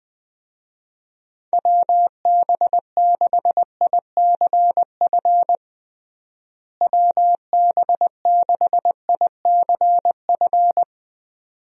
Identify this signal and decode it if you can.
Morse code, heard as a keyed tone, about 20 wpm, 700 Hz. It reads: WB6ICF WB6ICF
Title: USA Call Signs - 6 Characters - Repeated 2x - 20wpm